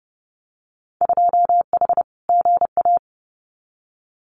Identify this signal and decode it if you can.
Morse code, heard as a keyed tone, about 30 wpm, 700 Hz. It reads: 2H GA
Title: Winter Field Day Exchanges Encore 30wpm